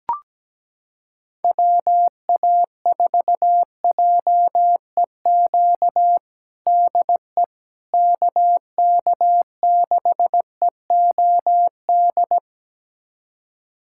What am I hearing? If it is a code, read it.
Morse code, heard as a keyed tone, about 17 wpm, 700 Hz. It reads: WA4JEQ DE KK6EOD
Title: USA Callsign DE Callsign 17wpm